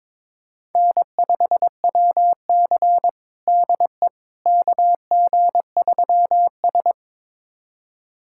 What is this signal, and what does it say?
Morse code, heard as a keyed tone, about 23 wpm, 700 Hz. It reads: N5WC DE KG3S
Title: USA Callsign DE Callsign - 4 Characters - 22wpm